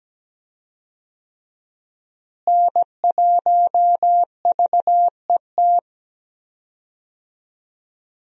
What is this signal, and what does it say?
Morse code, heard as a keyed tone, about 17 wpm, 700 Hz. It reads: N1VET
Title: USA Call Signs - 5 Characters - Repeated 3x - 17wpm